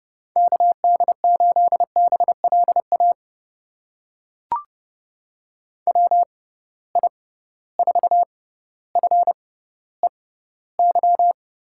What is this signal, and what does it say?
Morse code, heard as a keyed tone, about 29 wpm, spaced as for 15 wpm, 700 Hz. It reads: KD8BLA WI4FEY
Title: USA Call Signs - 6 Characters - 6x Character Spacing - 30wpm